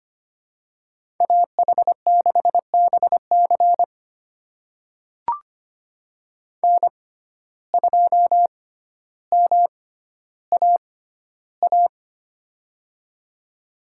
Morse code, heard as a keyed tone, about 25 wpm, spaced as for 13 wpm, 700 Hz. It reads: AH6BC N2MAA
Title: USA Call Signs - 5 Characters - 6x Character Spacing - 25wpm